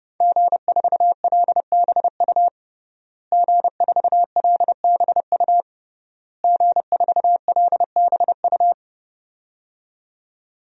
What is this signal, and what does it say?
Morse code, heard as a keyed tone, about 32 wpm, 700 Hz. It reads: G4LBU G4LBU G4LBU
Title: DXCC-100 Call Signs - 5 Characters - Repeated 3x - 30wpm